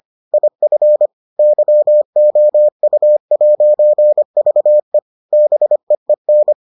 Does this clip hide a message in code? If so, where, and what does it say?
Yes: Morse code from 0.3 to 6.5 s, heard as a keyed tone, about 25 wpm, 600 Hz. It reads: IF YOU'VE BEEN